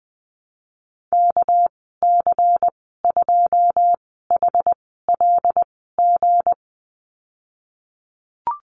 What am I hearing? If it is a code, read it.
Morse code, heard as a keyed tone, about 20 wpm, spaced as for 15 wpm, 700 Hz. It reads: KC2HLG
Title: 231 Lesson 37g - Letter x - Call Signs 20wpm